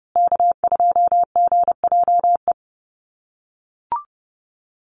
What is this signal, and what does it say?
Morse code, heard as a keyed tone, about 30 wpm, 700 Hz. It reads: K2GJE
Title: USA Call Signs Repeated 2x 30wpm